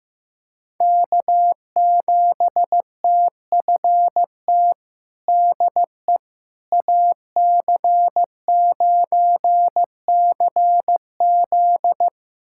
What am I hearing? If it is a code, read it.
Morse code, heard as a keyed tone, about 15 wpm, 700 Hz. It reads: K7TFT DE AC9CZ
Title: USA Callsign DE Callsign - 5 Characters - 15wpm